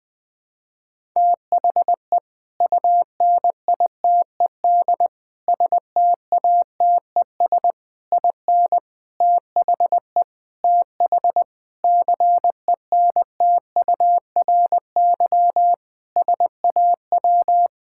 Morse code, heard as a keyed tone, about 20 wpm, 700 Hz. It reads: THE UNITED STATES IN THE TH CENTURY SAW
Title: Sentences with 5-Word Phrases - Spoken Beginning - 20wpm